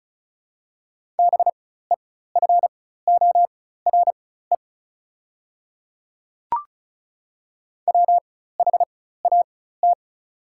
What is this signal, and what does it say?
Morse code, heard as a keyed tone, about 36 wpm, spaced as for 16 wpm, 690 Hz. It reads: BEFORE WHAT
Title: 4x Character Spacing - Top 300 Words - 35wpm